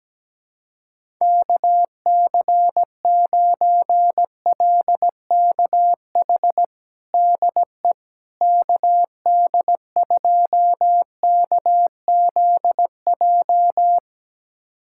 Morse code, heard as a keyed tone, about 17 wpm, 700 Hz. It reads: KC9LKH DE KD2KZJ
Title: USA Callsign DE Callsign 17wpm